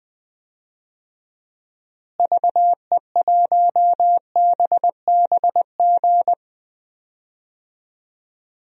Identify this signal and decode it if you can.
Morse code, heard as a keyed tone, about 20 wpm, 700 Hz. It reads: VE1BBG
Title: DXCC-100 Call Signs - 6 Characters - Repeated 3x - 20wpm